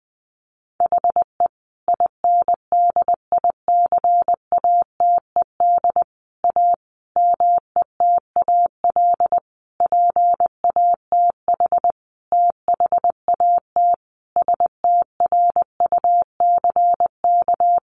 Morse code, heard as a keyed tone, about 20 wpm, 700 Hz. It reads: HE INDICATED A METAL PATH THAT STRUCK
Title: A Sound Of Thunder 20wpm